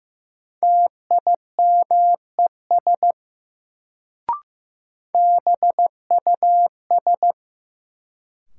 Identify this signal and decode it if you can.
Morse code, heard as a keyed tone, about 15 wpm, 700 Hz. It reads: TIMES BUS
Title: Sentences from Top 1000 Words - Review Part 6 - New Word Drill - 15wpm